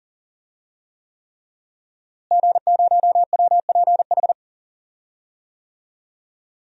Morse code, heard as a keyed tone, about 40 wpm, 700 Hz. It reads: G0WPH
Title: DXCC-100 Call Signs - 5 Characters - Repeated 3x - 40wpm